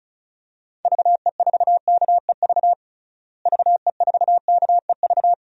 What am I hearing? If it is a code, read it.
Morse code, heard as a keyed tone, about 36 wpm, 700 Hz. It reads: VE4KEV VE4KEV
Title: DXCC-100 Call Signs - 6 Characters - Repeated 3x - 35wpm